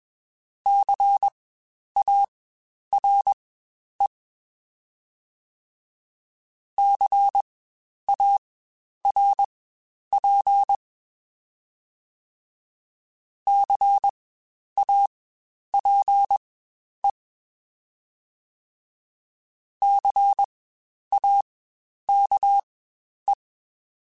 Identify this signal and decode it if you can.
Morse code, heard as a keyed tone, about 21 wpm, spaced as for 10 wpm, 800 Hz. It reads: CARE CARP CAPE CAKE
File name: Spaced Alphas / Spaced Alphas mp3s / Cs sp @20